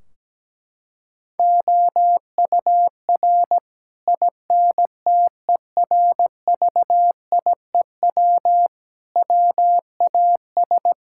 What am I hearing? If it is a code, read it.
Morse code, heard as a keyed tone, about 17 wpm, 700 Hz. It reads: OUR INTERVIEW WAS
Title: Sentences from Top 1600 Words 17wpm